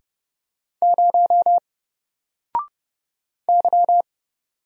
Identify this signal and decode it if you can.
Morse code, heard as a keyed tone, about 30 wpm, 700 Hz. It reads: 0 Y
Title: Single Character (Letter:Number:Punctuation:Prosign) 30wpm